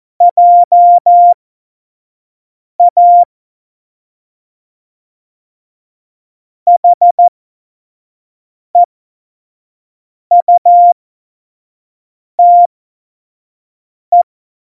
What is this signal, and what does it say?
Morse code, heard as a keyed tone, about 14 wpm, spaced as for 5 wpm, 700 Hz. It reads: JA HEUTE
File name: Morsekurs / Witze / Witz 22. Mai 2018, 25 F70